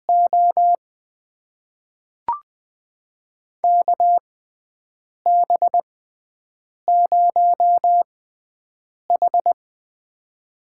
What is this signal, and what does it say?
Morse code, heard as a keyed tone, about 20 wpm, spaced as for 7 wpm, 700 Hz. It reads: O KB0H
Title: USA Call Signs - 4 Characters - 6x Character Spacing - 20wpm